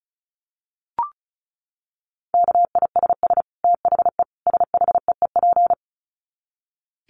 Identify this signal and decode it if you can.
Morse code, heard as a keyed tone, about 34 wpm, 700 Hz. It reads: KISS THE SHEEP
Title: Dit Challenge 35wpm